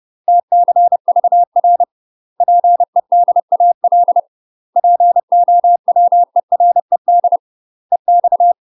Morse code, heard as a keyed tone, about 30 wpm, 700 Hz. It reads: TCVR PEDAL POWERED EX
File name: UK MASTER QSOs TRAINING / 30WPM / UK MASTER QSOs 0019